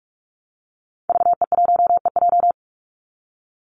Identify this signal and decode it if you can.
Morse code, heard as a keyed tone, about 44 wpm, 700 Hz. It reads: VE1EJ
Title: Call Signs Repeated 3x - 45wpm